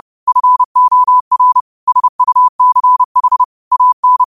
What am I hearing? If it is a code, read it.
Morse code, heard as a keyed tone, about 30 wpm, 1000 Hz. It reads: FOR SUCH AN
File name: qotdfiles / 30wpm